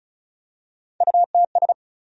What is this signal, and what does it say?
Morse code, heard as a keyed tone, about 36 wpm, 700 Hz. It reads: UTS
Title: USA Call Sign Postfixes - 3 Letters 35wpm